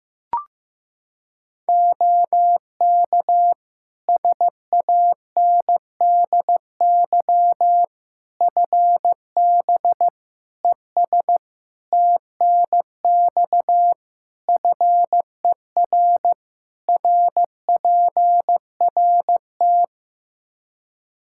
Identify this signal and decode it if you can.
Morse code, heard as a keyed tone, about 15 wpm, 700 Hz. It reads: OK SANDY FB ES TNX FER RPRT
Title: K1USN SST QSO Practice 15wpm